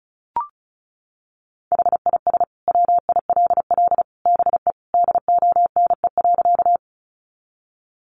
Morse code, heard as a keyed tone, about 34 wpm, 700 Hz. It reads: HIS WILL BE DONE.
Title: Sentences from Top 500 Words 35wpm